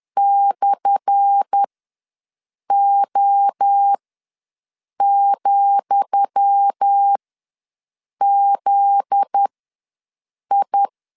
Morse code, heard as a keyed tone, about 10 wpm, spaced as for 6 wpm, 800 Hz. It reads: /O,ZI
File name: LEZIONI / 10 SIGNS / 10D COMA ws 10wpm 4314